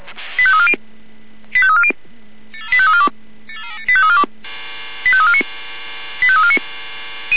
Five-tone selective call (ZVEI), heard as quick runs of five tones, tones of about 70 ms. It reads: 85340 85340 85342 85349 85349